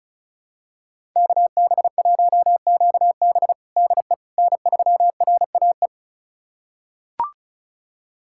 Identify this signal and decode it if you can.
Morse code, heard as a keyed tone, about 34 wpm, 700 Hz. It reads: KB1QB DE N3RAE
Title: USA Callsign DE Callsign - 5 Characters - 35wpm